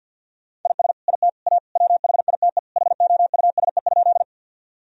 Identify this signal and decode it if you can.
Morse code, heard as a keyed tone, about 47 wpm, 700 Hz. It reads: IS IT A WHITE HOUSE?
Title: Sentences from Top 1000 Words - Review Part 1 - 50wpm